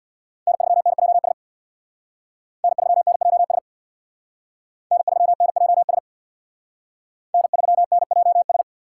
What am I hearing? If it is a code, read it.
Morse code, heard as a keyed tone, about 62 wpm, 700 Hz. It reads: N3NJS N3NJS N3NJS N3NJS
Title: Speed Racing - USA 1x3 Call Signs 50wpm